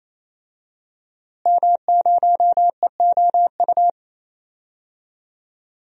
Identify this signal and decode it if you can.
Morse code, heard as a keyed tone, about 28 wpm, 700 Hz. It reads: M0EOU
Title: DXCC-100 Call Signs - 5 Characters - Repeated 3x - 28wpm